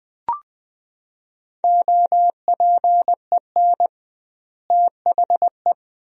Morse code, heard as a keyed tone, about 20 wpm, 700 Hz. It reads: OPEN THE
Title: 2x Word-Spacing - Sentences from Top 300 Words - 20wpm